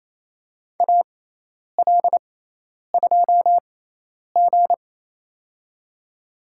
Morse code, heard as a keyed tone, about 28 wpm, spaced as for 10 wpm, 700 Hz. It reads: AL2G
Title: USA Call Signs - 4 Characters - 6x Character Spacing - 28wpm